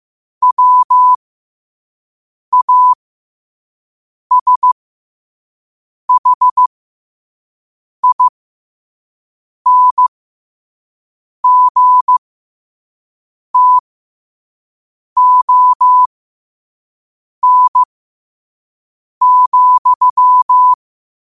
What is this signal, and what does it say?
Morse code, heard as a keyed tone, about 15 wpm, spaced as for 5 wpm, 1000 Hz. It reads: WASHINGTON,
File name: qotdfiles / 5wpm